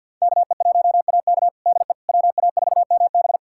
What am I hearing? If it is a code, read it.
Morse code, heard as a keyed tone, about 47 wpm, 700 Hz. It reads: KE1AK DE WA4MB
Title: USA Callsign DE Callsign - 5 Characters - 50wpm